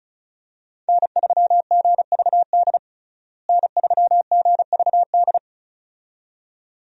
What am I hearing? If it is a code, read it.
Morse code, heard as a keyed tone, about 36 wpm, 700 Hz. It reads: N3GVD N3GVD
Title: USA Call Signs - 5 Characters - Repeated 2x - 35wpm